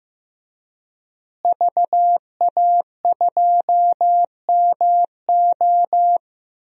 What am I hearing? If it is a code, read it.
Morse code, heard as a keyed tone, about 15 wpm, 700 Hz. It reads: VA2MO
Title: DXCC-100 Call Signs - 5 Characters - Repeated 3x - 15wpm